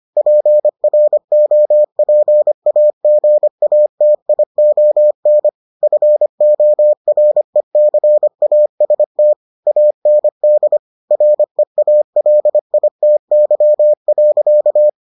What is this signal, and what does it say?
Morse code, heard as a keyed tone, about 25 wpm, 600 Hz. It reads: PROPAGATION FORECAST AND REALITY.